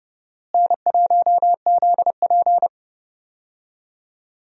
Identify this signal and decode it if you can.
Morse code, heard as a keyed tone, about 29 wpm, 700 Hz. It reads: N1ZP
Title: USA Call Signs - 4 Characters - Repeated 3x - 30wpm